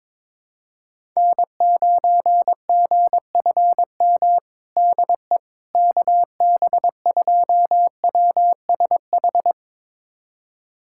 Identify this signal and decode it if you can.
Morse code, heard as a keyed tone, about 22 wpm, 700 Hz. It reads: N9GFM DE KB2WSH
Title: USA Callsign DE Callsign 22wpm